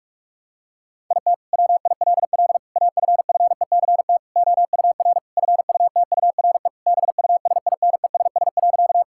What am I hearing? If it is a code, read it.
Morse code, heard as a keyed tone, about 41 wpm, 700 Hz. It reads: IT WILL AFFECT OUR FUTURE BUSINESS.
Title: Sentences from Top 1000 Words - Review Part 6 - 45wpm